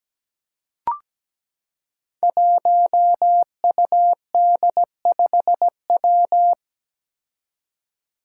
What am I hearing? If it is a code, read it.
Morse code, heard as a keyed tone, about 17 wpm, 700 Hz. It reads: 1UD5W
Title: Five Challenging Letter-Number Combinations 17wpm